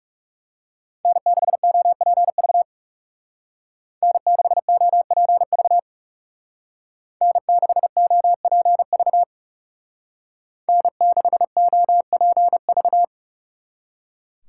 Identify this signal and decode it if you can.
Morse code, heard as a keyed tone, about 34 wpm, 700 Hz. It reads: N6OPV N6OPV N6OPV N6OPV
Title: Speed Racing - USA 1x3 Call Signs 30wpm